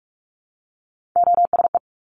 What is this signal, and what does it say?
Morse code, heard as a keyed tone, about 46 wpm, 700 Hz. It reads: OSE
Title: Top Three Letter Combinations in Words 45wpm